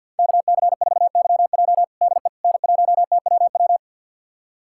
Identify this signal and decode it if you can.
Morse code, heard as a keyed tone, about 53 wpm, 700 Hz. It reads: KC4YJ DE N1TWW
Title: USA Callsign DE Callsign - 5 Characters - 50wpm